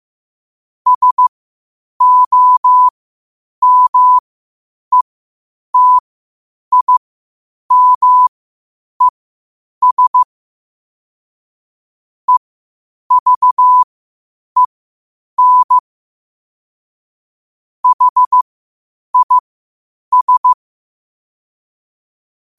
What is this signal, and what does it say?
Morse code, heard as a keyed tone, about 15 wpm, spaced as for 8 wpm, 1000 Hz. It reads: SOMETIMES EVEN HIS
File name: qotdfiles / 8wpm